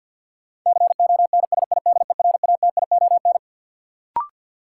Morse code, heard as a keyed tone, about 52 wpm, 700 Hz. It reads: CONSIDERATION
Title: Sentences from Top 1500 Words - New Word Drill - 50wpm